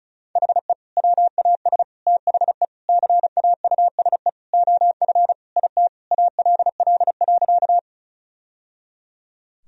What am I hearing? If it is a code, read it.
Morse code, heard as a keyed tone, about 34 wpm, 700 Hz. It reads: HE WAS THE CAUSE OF IT ALL.
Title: Sentences from Top 1000 Words - Review Part 1 - 35wpm